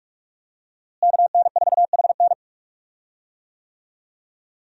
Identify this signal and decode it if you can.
Morse code, heard as a keyed tone, about 46 wpm, 700 Hz. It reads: KN4HN
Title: USA Call Signs - 5 Characters - Repeated 3x - 45wpm